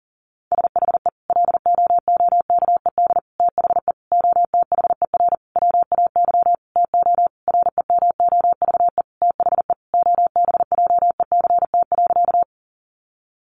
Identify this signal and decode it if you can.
Morse code, heard as a keyed tone, about 40 wpm, 700 Hz. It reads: SHE LOOKED THE OTHER WAY TO REMOVE THE OBJECT.
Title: Sentences from Top 800 Words 40wpm